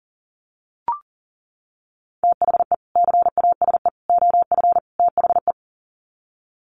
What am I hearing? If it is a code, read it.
Morse code, heard as a keyed tone, about 40 wpm, 700 Hz. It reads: THE CASE OF THE
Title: Sets of 4 Words 40wpm